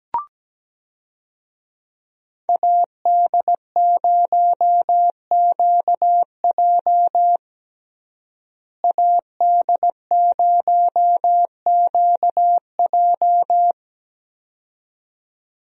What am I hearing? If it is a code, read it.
Morse code, heard as a keyed tone, about 17 wpm, 700 Hz. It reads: AD0QJ AD0QJ
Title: USA Call Signs - 5 Characters - Repeated 2x - 17wpm